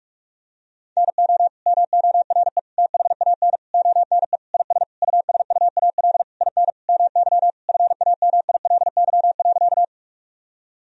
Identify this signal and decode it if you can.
Morse code, heard as a keyed tone, about 42 wpm, 700 Hz. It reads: NO MORE THAN ONE IS USUAL IN MY FAMILY.
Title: Sentences from Top 1000 Words - Review Part 2 - 45wpm